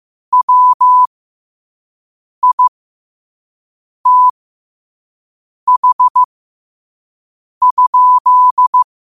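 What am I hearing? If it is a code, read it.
Morse code, heard as a keyed tone, about 15 wpm, spaced as for 5 wpm, 1000 Hz. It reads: WITH?
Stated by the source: Morse quote of the day at 5 wpm.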